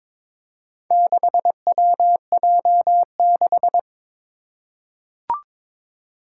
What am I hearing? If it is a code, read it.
Morse code, heard as a keyed tone, about 23 wpm, 700 Hz. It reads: 6WJ6
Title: Four Challenging Letter-Number Combinations 22wpm